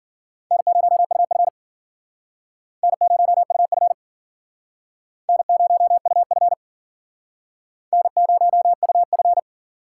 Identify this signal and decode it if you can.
Morse code, heard as a keyed tone, about 52 wpm, 700 Hz. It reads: N0UF N0UF N0UF N0UF